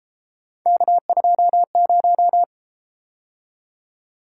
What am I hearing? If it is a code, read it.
Morse code, heard as a keyed tone, about 31 wpm, 700 Hz. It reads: K20